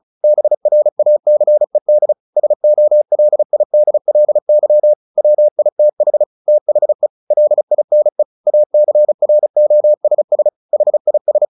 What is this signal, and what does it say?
Morse code, heard as a keyed tone, about 34 wpm, 600 Hz. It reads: BRACED SOLIDLY WITH THE LINE ACROSS HIS
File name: the old man and the sea / the old man and the sea 1 35wpm